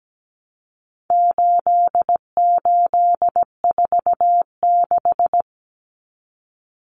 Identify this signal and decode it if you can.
Morse code, heard as a keyed tone, about 17 wpm, 700 Hz. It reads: 8846
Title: Numbers - 4-Digit 17wpm